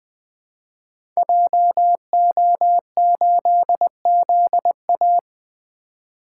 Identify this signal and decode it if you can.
Morse code, heard as a keyed tone, about 21 wpm, 700 Hz. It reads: JO8ZA
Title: Code Groups - 5 Letter:Numbers - 20wpm